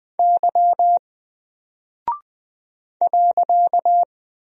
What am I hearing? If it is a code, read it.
Morse code, heard as a keyed tone, about 20 wpm, 700 Hz. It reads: Y .
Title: Single Character (Letter:Number:Punctuation:Prosign) 20wpm